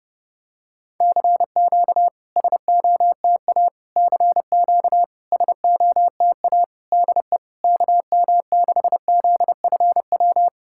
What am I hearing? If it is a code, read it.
Morse code, heard as a keyed tone, about 32 wpm, 690 Hz. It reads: CQ SOTA CQ SOTA DE KM6ZFW
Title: SOTA QSO Practice 30wpm